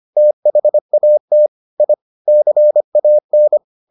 Morse code, heard as a keyed tone, about 25 wpm, 600 Hz. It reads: THAT I CAN